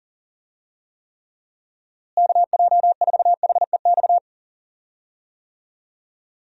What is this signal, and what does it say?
Morse code, heard as a keyed tone, about 40 wpm, 700 Hz. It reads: KJ4HEX
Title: USA Call Signs - 6 Characters - Repeated 2x - 40wpm